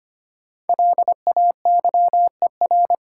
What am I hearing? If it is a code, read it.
Morse code, heard as a keyed tone, about 25 wpm, 700 Hz. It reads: LAYER
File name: 5 Letter Words - First 3 Letters Given - 25wpm